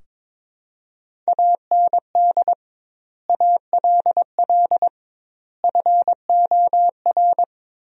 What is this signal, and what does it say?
Morse code, heard as a keyed tone, about 22 wpm, 700 Hz. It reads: AND ALL FOR
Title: 2x Word-Spacing - Sentences from Top 100 Words - 22wpm